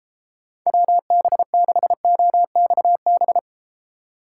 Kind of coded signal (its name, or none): Morse code